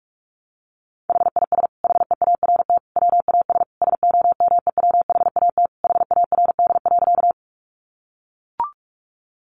Morse code, heard as a keyed tone, about 44 wpm, 700 Hz. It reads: HIS HEART WAS SOMEWHAT HARD.
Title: Sentences from Top 1300 Words 45wpm